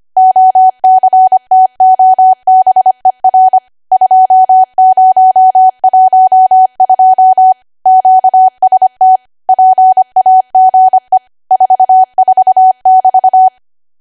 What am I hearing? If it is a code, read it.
Morse code, heard as a keyed tone, about 25 wpm, 750 Hz. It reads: OCTOBER 2012 QST PAGE 44<BT>
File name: Archive / 25 WPM / 25WPM